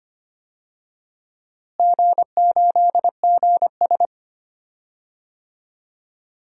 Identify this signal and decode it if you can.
Morse code, heard as a keyed tone, about 25 wpm, 700 Hz. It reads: G8GS
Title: DXCC-100 Call Signs - 4 Characters - Repeated 3x - 25wpm